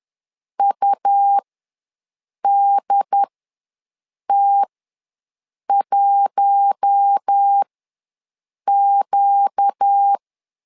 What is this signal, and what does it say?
Morse code, heard as a keyed tone, about 10 wpm, spaced as for 6 wpm, 800 Hz. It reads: UDT1Q
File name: LEZIONI / 4 Q1 ws 10wpm 2538